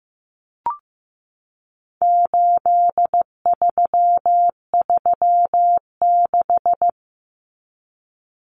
Morse code, heard as a keyed tone, about 15 wpm, 700 Hz. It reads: 8336
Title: Numbers - 4-Digit 15wpm